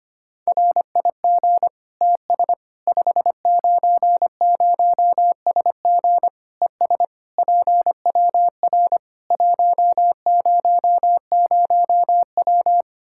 Morse code, heard as a keyed tone, about 25 wpm, 700 Hz. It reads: RIG TS 590SG ES PWR 100W
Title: K1USN SST QSO Practice 25wpm